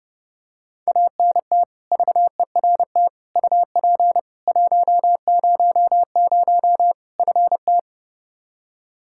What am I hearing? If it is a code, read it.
Morse code, heard as a keyed tone, about 30 wpm, 700 Hz. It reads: ANT VERT UP 100 FT
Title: Practice QSO - Exchange 2 - Info - 30wpm